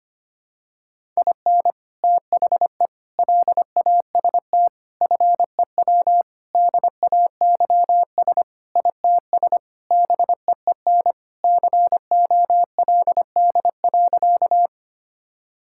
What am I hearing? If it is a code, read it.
Morse code, heard as a keyed tone, about 25 wpm, 700 Hz. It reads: IN THE LAST FEW DAYS ITS BEEN COLD.
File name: practice / 25wpm / Sentences with 3-Word Phrases - 25wpm